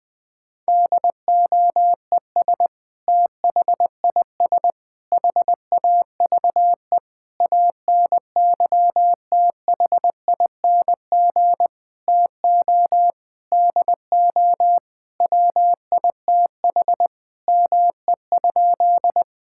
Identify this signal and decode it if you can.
Morse code, heard as a keyed tone, about 21 wpm, 700 Hz. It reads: DOES THIS HAVE ANYTHING TO DO WITH ME?
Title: Sentences with 5-Word Phrases - 20wpm